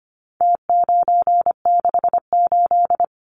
Morse code, heard as a keyed tone, about 25 wpm, 700 Hz. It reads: T968
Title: Numbers - Contest Style - 4-Digit 25wpm